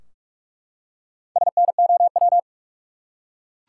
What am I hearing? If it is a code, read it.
Morse code, heard as a keyed tone, about 42 wpm, 700 Hz. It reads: SNOW
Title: Four Letter Words - 45wpm